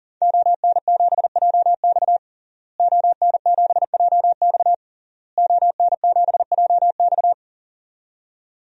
Morse code, heard as a keyed tone, about 40 wpm, 700 Hz. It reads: ON7JX ON7JX ON7JX
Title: DXCC-100 Call Signs - 5 Characters - Repeated 3x - 40wpm